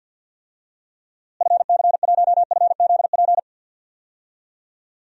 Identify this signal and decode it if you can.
Morse code, heard as a keyed tone, about 52 wpm, 700 Hz. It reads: FC1FZP
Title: DXCC-100 Call Signs - 6 Characters - Repeated 3x - 50wpm